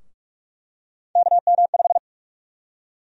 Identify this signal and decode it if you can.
Morse code, heard as a keyed tone, about 42 wpm, 700 Hz. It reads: KM5
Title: USA Call Sign Prefixes - 45wpm